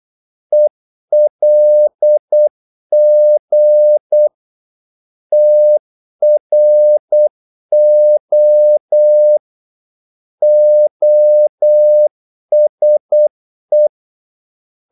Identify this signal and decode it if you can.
Morse code, heard as a keyed tone, about 8 wpm, 600 Hz. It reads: ELG TRO OSE
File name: Korte ord DK 8wpm